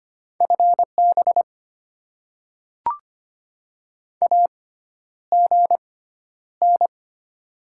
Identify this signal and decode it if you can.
Morse code, heard as a keyed tone, about 25 wpm, spaced as for 11 wpm, 700 Hz. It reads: FB AGN
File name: Top 20 QSO Elements - 6x Character Spacing - 25wpm